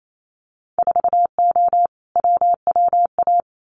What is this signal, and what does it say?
Morse code, heard as a keyed tone, about 28 wpm, 700 Hz. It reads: 4O WWA
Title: Winter Field Day Exchanges 28wpm